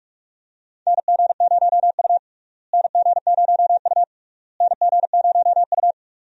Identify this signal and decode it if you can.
Morse code, heard as a keyed tone, about 46 wpm, 700 Hz. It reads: NG0U NG0U NG0U
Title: USA Call Signs - 4 Characters - Repeated 3x - 45wpm